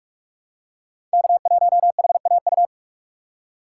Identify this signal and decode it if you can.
Morse code, heard as a keyed tone, about 46 wpm, 700 Hz. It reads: K1HAU
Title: USA Call Signs - 5 Characters - 3x Character Spacing - 45wpm